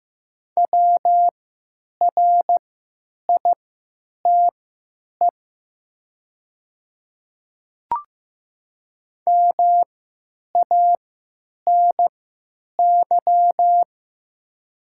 Morse code, heard as a keyed tone, about 15 wpm, spaced as for 9 wpm, 700 Hz. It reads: WRITE MANY
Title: 3x Character Spacing - Top 100 Words - 15wpm